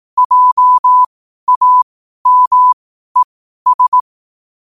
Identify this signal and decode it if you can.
Morse code, heard as a keyed tone, about 18 wpm, spaced as for 12 wpm, 1000 Hz. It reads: JAMES
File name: qotdfiles / 12wpm